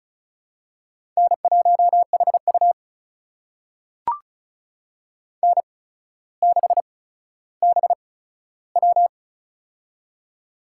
Morse code, heard as a keyed tone, about 36 wpm, spaced as for 15 wpm, 700 Hz. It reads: N1HU N6BW
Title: USA Call Signs - 4 Characters - 8x Character Spacing - 35wpm